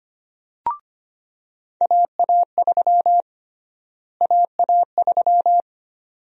Morse code, heard as a keyed tone, about 25 wpm, 700 Hz. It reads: AA3 AA3
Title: USA Call Sign Prefixes - Repeated 2x - 25wpm